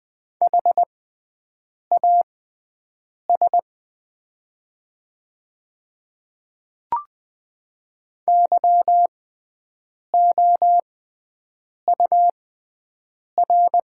Morse code, heard as a keyed tone, about 20 wpm, spaced as for 7 wpm, 700 Hz. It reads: HAS YOUR
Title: 6x Character Spacing - Top 100 Words - 20wpm